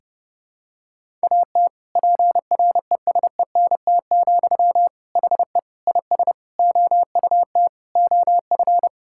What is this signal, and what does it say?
Morse code, heard as a keyed tone, about 32 wpm, 700 Hz. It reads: AT PRESENT, HE IS OUT OF
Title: Sentences from Top 1000 Words - Review Part 5 - 30wpm